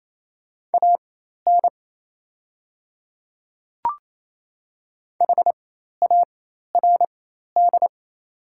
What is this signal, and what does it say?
Morse code, heard as a keyed tone, about 28 wpm, spaced as for 13 wpm, 700 Hz. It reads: AN HARD
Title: 4x Character Spacing - Top 300 Words - 28wpm